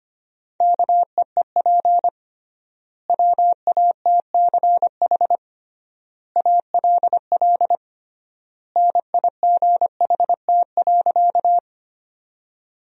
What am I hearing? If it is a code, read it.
Morse code, heard as a keyed tone, about 25 wpm, 700 Hz. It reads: KEEP WATCH ALL NIGHT.
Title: 3x Word-Spacing - Sentences from Top 300 Words - 25wpm